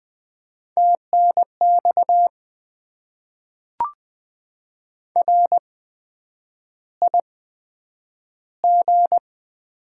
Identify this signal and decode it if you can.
Morse code, heard as a keyed tone, about 20 wpm, spaced as for 9 wpm, 700 Hz. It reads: TNX RIG
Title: Top 20 QSO Elements - 8x Character Spacing - 20wpm